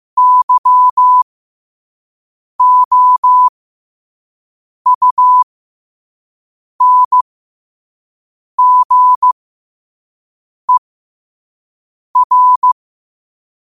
Morse code, heard as a keyed tone, about 15 wpm, spaced as for 5 wpm, 1000 Hz. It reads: YOUNGER
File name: 5wpm